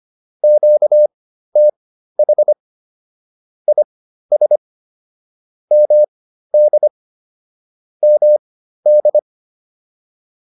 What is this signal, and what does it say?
Morse code, heard as a keyed tone, about 25 wpm, spaced as for 13 wpm, 600 Hz. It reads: QTH IS MD MD